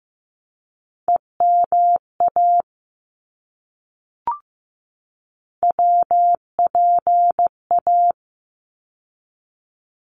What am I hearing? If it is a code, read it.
Morse code, heard as a keyed tone, about 15 wpm, 700 Hz. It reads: EMA WPA
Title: ARRL Sections 15wpm